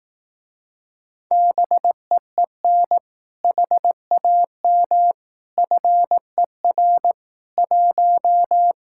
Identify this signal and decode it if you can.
Morse code, heard as a keyed tone, about 18 wpm, 700 Hz. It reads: BEEN HAM FER 1
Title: Speed-Racing Common QSO Phrases Encore - 15wpm